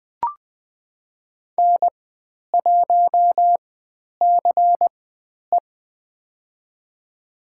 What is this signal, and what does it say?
Morse code, heard as a keyed tone, about 20 wpm, spaced as for 10 wpm, 700 Hz. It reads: N1CE
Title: 095 Lesson 16g - Letter c - Call Signs 20wpm